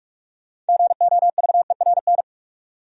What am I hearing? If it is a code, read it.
Morse code, heard as a keyed tone, about 57 wpm, 690 Hz. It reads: GOVERN